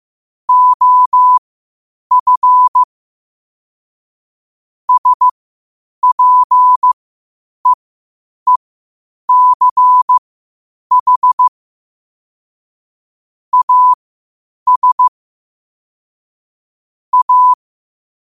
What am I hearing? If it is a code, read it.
Morse code, heard as a keyed tone, about 15 wpm, spaced as for 8 wpm, 1000 Hz. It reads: OF SPEECH AS A
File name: qotdfiles / 8wpm